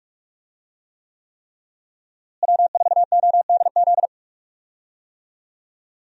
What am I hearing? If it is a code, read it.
Morse code, heard as a keyed tone, about 46 wpm, 700 Hz. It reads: W4ODZ
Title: USA Call Signs - 5 Characters - Repeated 3x - 45wpm